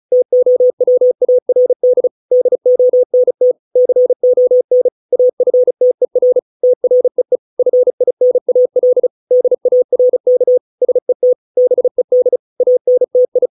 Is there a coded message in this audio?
Yes: Morse code from 0.1 to 13.5 s, heard as a keyed tone, about 34 wpm, 500 Hz. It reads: TOWARD DONT CON AFTER TREE FINAL DARK SET BED ANTI